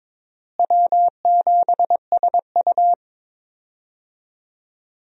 Morse code, heard as a keyed tone, about 22 wpm, 700 Hz. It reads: W7SU
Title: USA Call Signs - 4 Characters - Repeated 3x - 22wpm